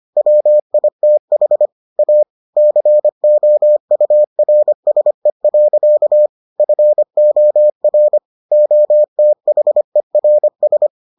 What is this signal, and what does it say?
Morse code, heard as a keyed tone, about 25 wpm, 600 Hz. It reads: WITH A COURSE. FOR OTHERS